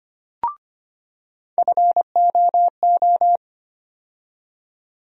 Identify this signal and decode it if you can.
Morse code, heard as a keyed tone, about 25 wpm, 700 Hz. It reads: FOO
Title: USA Call Sign Postfixes - 3 Letters 25wpm